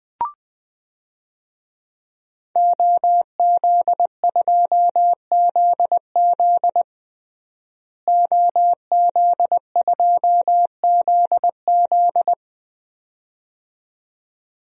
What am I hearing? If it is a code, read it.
Morse code, heard as a keyed tone, about 20 wpm, 700 Hz. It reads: OZ2ZZ OZ2ZZ
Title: DXCC-100 Call Signs - 5 Characters - Repeated 2x - 20wpm